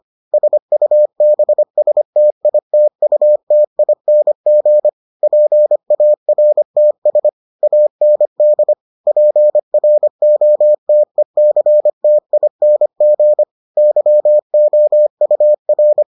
Morse code, heard as a keyed tone, about 25 wpm, 600 Hz. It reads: SUBSTITUTING PARTS AND PROTECTING YOUR